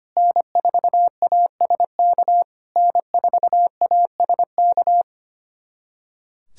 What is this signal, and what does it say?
Morse code, heard as a keyed tone, about 25 wpm, 690 Hz. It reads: N4ASK N4ASK
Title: SOTA QSO Practice 25wpm